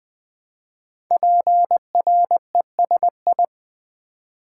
Morse code, heard as a keyed tone, about 20 wpm, 700 Hz. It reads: PRESI
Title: Build A Word - 5 Letters At A Time - 12-Letter Words - 20wpm